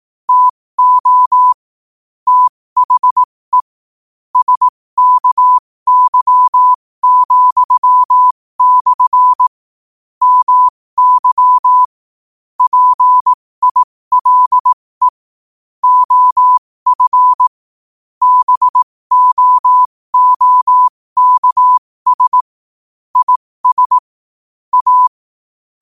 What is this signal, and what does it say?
Morse code, heard as a keyed tone, about 18 wpm, spaced as for 15 wpm, 1000 Hz. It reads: TO THE SKY,/ MY PILE OF BOOKS IS A
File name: qotdfiles / 15wpm